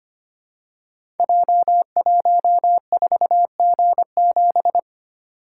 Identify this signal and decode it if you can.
Morse code, heard as a keyed tone, about 25 wpm, 700 Hz. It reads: J14G7
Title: Code Groups - 5 Letter:Numbers - 25wpm